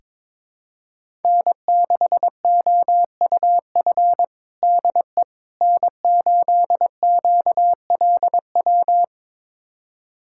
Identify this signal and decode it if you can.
Morse code, heard as a keyed tone, about 22 wpm, 700 Hz. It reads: N6OUF DE N8QLW
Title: USA Callsign DE Callsign - 5 Characters - 22wpm